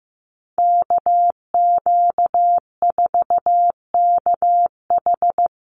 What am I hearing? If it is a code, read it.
Morse code, heard as a keyed tone, about 15 wpm, 700 Hz. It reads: KQ4KH
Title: USA Call Signs Repeated 2x 15wpm